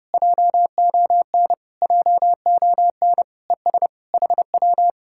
Morse code, heard as a keyed tone, about 33 wpm, 700 Hz. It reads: JON JON ES HW